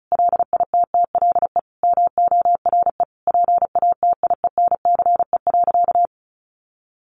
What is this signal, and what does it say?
Morse code, heard as a keyed tone, about 34 wpm, 700 Hz. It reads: LITTLE MORE PATIENCE.
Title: Sentences from Top 1300 Words 35wpm